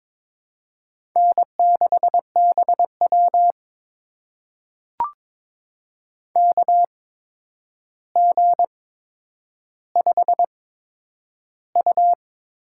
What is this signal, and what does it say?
Morse code, heard as a keyed tone, about 22 wpm, spaced as for 9 wpm, 700 Hz. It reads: N6BW KG5U
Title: USA Call Signs - 4 Characters - 8x Character Spacing - 22wpm